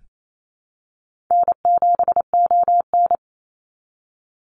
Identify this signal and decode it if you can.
Morse code, heard as a keyed tone, about 28 wpm, 700 Hz. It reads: N7ON
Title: Easy Call Signs - 28wpm